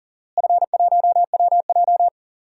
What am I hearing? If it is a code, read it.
Morse code, heard as a keyed tone, about 38 wpm, 700 Hz. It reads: F1WJ
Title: DXCC-100 Call Signs - 4 Characters - Repeated 3x - 40wpm